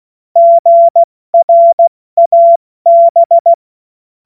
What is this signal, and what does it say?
Morse code, heard as a keyed tone, about 16 wpm, 680 Hz. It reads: GRAB
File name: Morsekurs / Witze / Witz 06. September 2022 70 F80